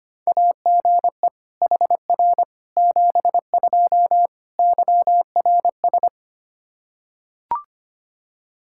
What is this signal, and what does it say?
Morse code, heard as a keyed tone, about 25 wpm, 700 Hz. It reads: AGE HR 72 YRS
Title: Practice QSO - Exchange 3 - More Info - 25wpm